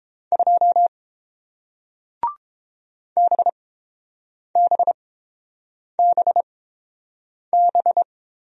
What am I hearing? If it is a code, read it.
Morse code, heard as a keyed tone, about 30 wpm, spaced as for 7 wpm, 700 Hz. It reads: 2 BBBB